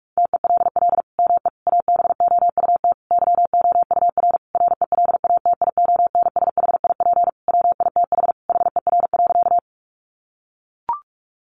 Morse code, heard as a keyed tone, about 44 wpm, 700 Hz. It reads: TELL ME ABOUT YOUR RELATIONSHIP WITH HER.
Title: Sentences from Top 1100 Words 45wpm